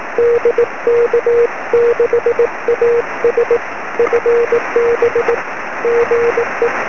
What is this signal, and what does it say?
Morse code, heard as a keyed tone, about 18 wpm, 470 Hz, with noise. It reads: DK6AS FB GE
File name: sm7lcb / 07-dk6as 23cm